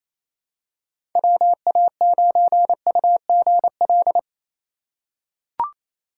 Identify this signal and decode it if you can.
Morse code, heard as a keyed tone, about 28 wpm, 700 Hz. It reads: WA9UGL
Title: USA Call Signs - 6x Character Spacing - 28wpm